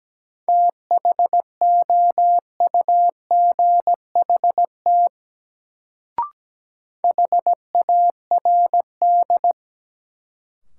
Morse code, heard as a keyed tone, about 17 wpm, 700 Hz. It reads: THOUGHT HARD
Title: Sentences from Top 300 Words - New Word Drill - 17wpm